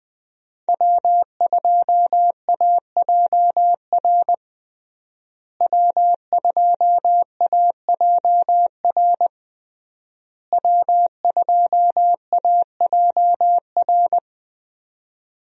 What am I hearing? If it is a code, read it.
Morse code, heard as a keyed tone, about 20 wpm, 700 Hz. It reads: W2AJR W2AJR W2AJR
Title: USA Call Signs - 5 Characters - Repeated 3x - 20wpm